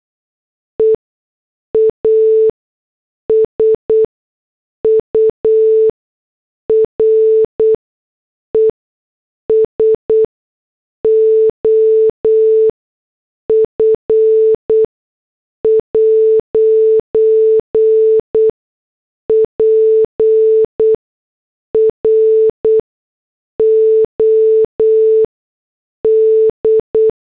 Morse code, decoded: EASURES OF 'PROD